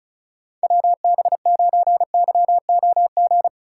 Morse code, heard as a keyed tone, about 36 wpm, 700 Hz. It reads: WB9YOG
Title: USA Call Signs - 6 Characters - 4x Character Spacing - 35wpm